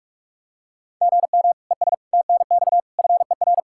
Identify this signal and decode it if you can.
Morse code, heard as a keyed tone, about 41 wpm, 700 Hz. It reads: GM ES TNX FER
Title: K1USN SST QSO Practice 45wpm